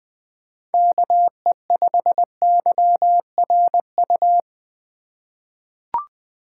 Morse code, heard as a keyed tone, about 20 wpm, 700 Hz. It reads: KE5YRU
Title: USA Call Signs - 6 Characters - 4x Character Spacing - 20wpm